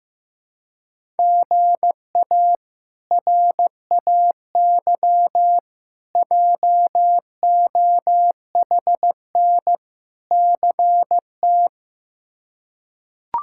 Morse code, heard as a keyed tone, about 16 wpm, 700 Hz. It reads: GA RAY JOHN CT
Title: K1USN SST QSO Practice - 15wpm